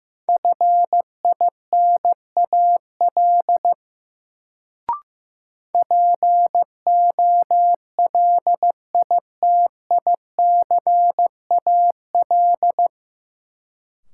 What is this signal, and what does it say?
Morse code, heard as a keyed tone, about 15 wpm, 700 Hz. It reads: FINAL POLITICAL
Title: Sentences from Top 800 Words - New Word Drill - 15wpm